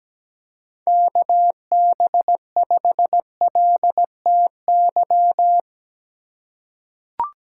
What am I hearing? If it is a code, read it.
Morse code, heard as a keyed tone, about 17 wpm, 700 Hz. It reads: KB5LTY
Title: USA Call Signs - 6 Characters - 2x Character Spacing - 17wpm